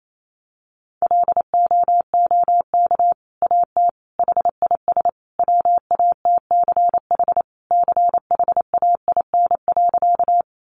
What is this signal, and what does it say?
Morse code, heard as a keyed tone, about 28 wpm, 700 Hz. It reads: LOOK AT HIS WATCH CHAIN.
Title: Sentences from Top 1500 Words 28wpm